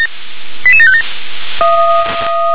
Five-tone selective call (ZVEI), heard as quick runs of five tones, tones of about 70 ms. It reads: 88757